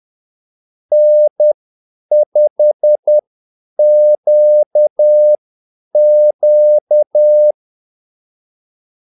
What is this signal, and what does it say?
Morse code, heard as a keyed tone, about 10 wpm, spaced as for 8 wpm, 600 Hz. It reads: N5QQ
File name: Callsigns-10-8wpm